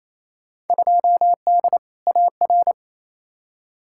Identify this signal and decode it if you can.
Morse code, heard as a keyed tone, about 28 wpm, 700 Hz. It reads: 2D AR
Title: Field Day Exchanges Encore - 28wpm